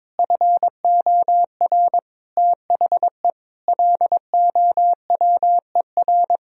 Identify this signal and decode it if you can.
Morse code, heard as a keyed tone, about 22 wpm, 700 Hz. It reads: FOR THE LOWER
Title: Sentences from Top 1000 Words - Review Part 4 - 22wpm